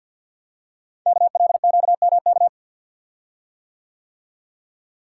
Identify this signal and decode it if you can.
Morse code, heard as a keyed tone, about 48 wpm, 700 Hz. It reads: KLQMK
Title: Code Groups - 5 Letters - 50wpm